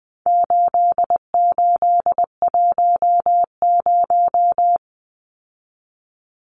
Morse code, heard as a keyed tone, about 20 wpm, 700 Hz. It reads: 8810
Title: Numbers - 4-Digit 20wpm